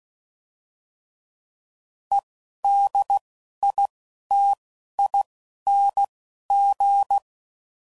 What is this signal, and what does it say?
Morse code, heard as a keyed tone, about 16 wpm, 800 Hz. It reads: E D I T I N G